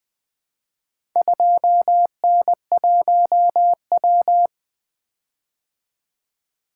Morse code, heard as a keyed tone, about 20 wpm, 700 Hz. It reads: 2N1W
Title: Four Letter-Number Combinations - 20wpm